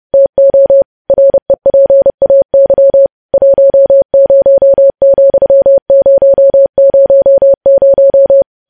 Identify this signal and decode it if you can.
Morse code, heard as a keyed tone, about 30 wpm, 550 Hz. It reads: TO REPAY 10,000
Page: news-headlines-in-morse-code-at-30-wpm